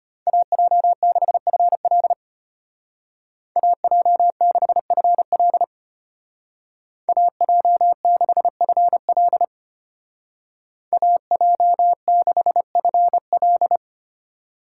Morse code, heard as a keyed tone, about 30 wpm, 700 Hz. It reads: AJ6FL AJ6FL AJ6FL AJ6FL
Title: Speed Racing - USA 2x2 Call Signs 25wpm